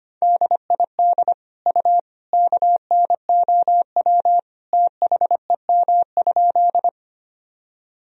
Morse code, heard as a keyed tone, about 25 wpm, 700 Hz. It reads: DID U KNOW THEM?
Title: Sentences from Top 100 Words and Abbreviations 25wpm